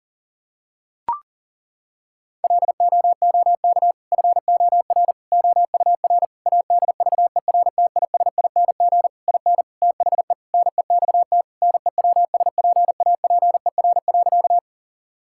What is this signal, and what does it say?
Morse code, heard as a keyed tone, about 37 wpm, 700 Hz. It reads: LOOK FOR OUR ADVERTISING IN THE NEXT NEWSPAPER.
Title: Sentences from Top 1000 Words - Review Part 6 - 40wpm